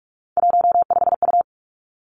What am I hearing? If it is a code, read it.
Morse code, heard as a keyed tone, about 46 wpm, 700 Hz. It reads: 15U
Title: Three Challenging Letter-Number Combinations 45wpm